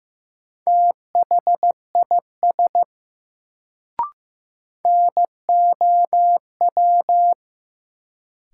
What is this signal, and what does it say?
Morse code, heard as a keyed tone, about 15 wpm, 700 Hz. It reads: THIS NOW
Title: Sentences from Top 100 Words - New Word Drill - 15wpm